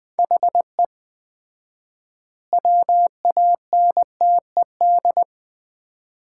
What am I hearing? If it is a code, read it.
Morse code, heard as a keyed tone, about 20 wpm, 700 Hz. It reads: HE WANTED
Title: 4x Word-Spacing - Sets of 3 Words - 20wpm